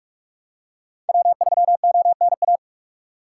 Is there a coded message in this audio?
Yes: Morse code from 1.1 to 2.6 s, heard as a keyed tone, about 48 wpm, 700 Hz. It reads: W3ONA